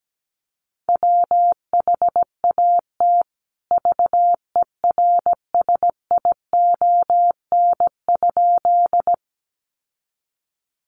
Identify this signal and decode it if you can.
Morse code, heard as a keyed tone, about 17 wpm, 700 Hz. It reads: WHAT VERSION?
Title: Sentences from Top 1400 Words 17wpm